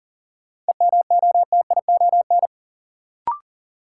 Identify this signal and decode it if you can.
Morse code, heard as a keyed tone, about 40 wpm, 700 Hz. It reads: EMOTION
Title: Sentences from Top 1600 Words - New Word Drill Alt - 40wpm